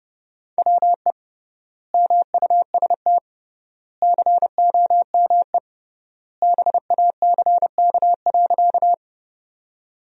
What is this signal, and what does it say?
Morse code, heard as a keyed tone, about 31 wpm, 700 Hz. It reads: WE MUST COME BACK.
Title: 3x Word-Spacing - Sentences from Top 200 Words - 30wpm